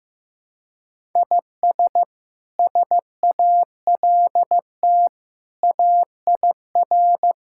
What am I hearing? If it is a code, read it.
Morse code, heard as a keyed tone, about 15 wpm, 700 Hz. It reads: IS SALT AIR
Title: Sentences from Top 1000 Words - Review Part 5 - 15wpm